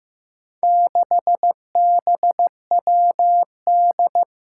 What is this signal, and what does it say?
Morse code, heard as a keyed tone, about 15 wpm, 700 Hz. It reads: 6BWD
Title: Four Challenging Letter-Number Combinations 15wpm